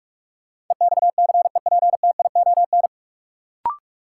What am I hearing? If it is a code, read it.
Morse code, heard as a keyed tone, about 45 wpm, 700 Hz. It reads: EXCEPTION
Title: Sets of 4 Words - New Word Drill - 45wpm